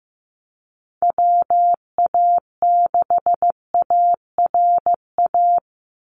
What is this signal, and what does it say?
Morse code, heard as a keyed tone, about 15 wpm, 700 Hz. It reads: WA6ARA
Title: NA SOTA Chaser Callsigns 15wpm